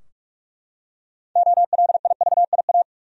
Morse code, heard as a keyed tone, about 42 wpm, 700 Hz. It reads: OLIVIA
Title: USA Common Names - 45wpm